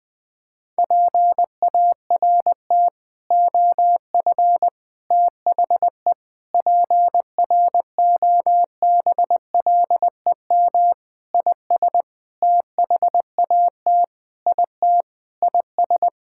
Morse code, heard as a keyed tone, about 20 wpm, 700 Hz. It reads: PART OF THE PROBLEM IS THAT IT IS
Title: Sentences with 6-Word Phrases - 20wpm